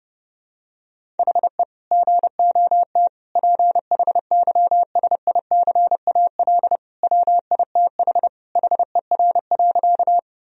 Morse code, heard as a keyed tone, about 32 wpm, 700 Hz. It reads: HE GOT PHYSICAL WITH HER.
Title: Sentences from Top 1000 Words - Review Part 4 - 30wpm